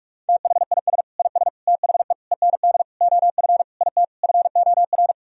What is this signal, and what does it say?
Morse code, heard as a keyed tone, about 46 wpm, 700 Hz. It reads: THIS IS THE END OF IT FOR
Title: Sentences from Top 1000 Words - Review Part 3 - 45wpm